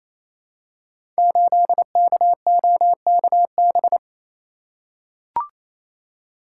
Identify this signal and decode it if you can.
Morse code, heard as a keyed tone, about 28 wpm, 700 Hz. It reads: 8KOKB
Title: Code Groups - 5 Letter:Numbers - 28wpm